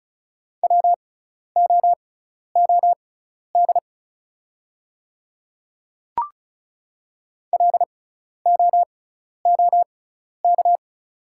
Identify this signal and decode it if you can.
Morse code, heard as a keyed tone, about 36 wpm, spaced as for 12 wpm, 690 Hz. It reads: WOOD LOOK
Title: 6x Character Spacing - Top 300 Words - 35wpm